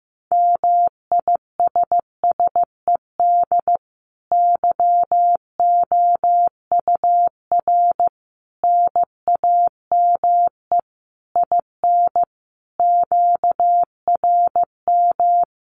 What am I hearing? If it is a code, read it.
Morse code, heard as a keyed tone, about 15 wpm, 700 Hz. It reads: MISSED YOUR NAME IN QRM
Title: Common QSO Phrases 15wpm